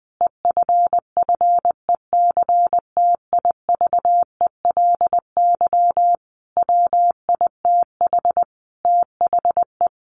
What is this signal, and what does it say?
Morse code, heard as a keyed tone, about 20 wpm, 700 Hz. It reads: EFFECTIVELY WITH THE
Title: Sentences from Top 1400 Words 20wpm